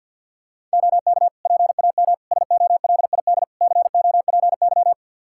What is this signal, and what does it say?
Morse code, heard as a keyed tone, about 51 wpm, 700 Hz. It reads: OK PAM SOLID COPY
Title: K1USN SST QSO Practice 50wpm